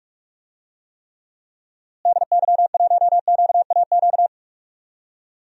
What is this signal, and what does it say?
Morse code, heard as a keyed tone, about 44 wpm, 700 Hz. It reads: DY1QAQ
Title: DXCC-100 Call Signs - 6 Characters - Repeated 2x - 45wpm